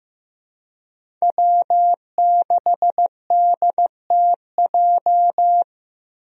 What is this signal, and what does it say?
Morse code, heard as a keyed tone, about 15 wpm, 700 Hz. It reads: W6DTJ
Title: USA Call Signs - 5 Characters - Repeated 2x - 15wpm